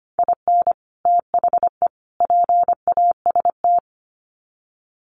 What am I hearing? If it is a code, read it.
Morse code, heard as a keyed tone, about 25 wpm, 700 Hz. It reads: IN THE PAST
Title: Sets of 3 Words 25wpm